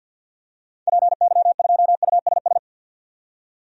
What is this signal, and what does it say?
Morse code, heard as a keyed tone, about 53 wpm, 700 Hz. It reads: PY2USS
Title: DXCC-100 Call Signs - 6 Characters - 50wpm